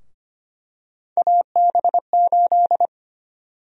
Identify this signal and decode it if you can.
Morse code, heard as a keyed tone, about 25 wpm, 700 Hz. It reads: AB8
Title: USA Call Sign Prefixes - Repeated 2x - 25wpm